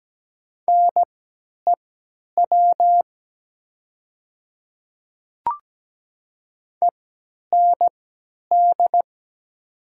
Morse code, heard as a keyed tone, about 17 wpm, spaced as for 10 wpm, 700 Hz. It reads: NEW END
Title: 3x Character Spacing - Top 200 Words - 17wpm